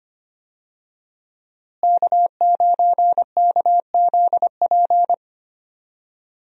Morse code, heard as a keyed tone, about 25 wpm, 700 Hz. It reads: K9KZP
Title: USA Call Signs - 5 Characters - Repeated 3x - 25wpm